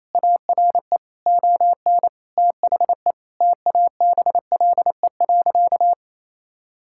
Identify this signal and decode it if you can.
Morse code, heard as a keyed tone, about 28 wpm, 700 Hz. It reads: ARE ON THE TABLE.
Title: Sentences from Top 1000 Words - Review Part 6 - 28wpm